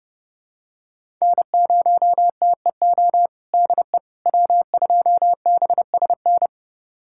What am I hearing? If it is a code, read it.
Morse code, heard as a keyed tone, about 32 wpm, 700 Hz. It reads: N0TEO DE W2BSN
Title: USA Callsign DE Callsign - 5 Characters - 30wpm